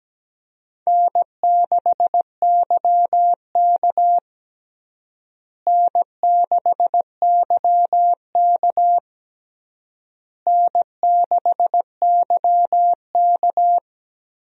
Morse code, heard as a keyed tone, about 17 wpm, 700 Hz. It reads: N6YK N6YK N6YK
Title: USA Call Signs - 4 Characters - Repeated 3x - 17wpm